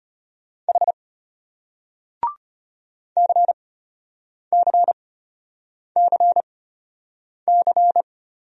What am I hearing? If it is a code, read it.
Morse code, heard as a keyed tone, about 33 wpm, spaced as for 8 wpm, 700 Hz. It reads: H CCCC